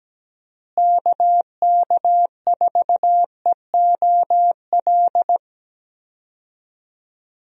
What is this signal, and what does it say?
Morse code, heard as a keyed tone, about 17 wpm, 700 Hz. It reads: KK4EOL
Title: USA Call Signs - 6 Characters - Repeated 3x - 17wpm